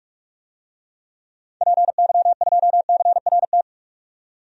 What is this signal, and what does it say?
Morse code, heard as a keyed tone, about 46 wpm, 700 Hz. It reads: PY2CRT
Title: DXCC-100 Call Signs - 6 Characters - 45wpm